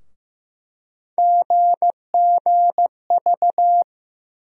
Morse code, heard as a keyed tone, about 15 wpm, 700 Hz. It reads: GGV
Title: USA Call Sign Postfixes - 3 Letters 15wpm